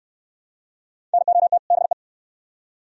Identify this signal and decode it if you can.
Morse code, heard as a keyed tone, about 66 wpm, 700 Hz. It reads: NOT BE